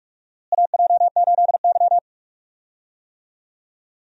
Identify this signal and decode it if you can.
Morse code, heard as a keyed tone, about 46 wpm, 700 Hz. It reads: AJ8Y
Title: USA Call Signs - 4 Characters - Repeated 3x - 45wpm